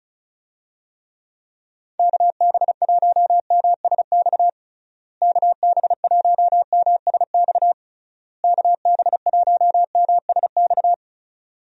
Morse code, heard as a keyed tone, about 34 wpm, 700 Hz. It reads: KB1MSX KB1MSX KB1MSX
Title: USA Call Signs - 6 Characters - Repeated 3x - 35wpm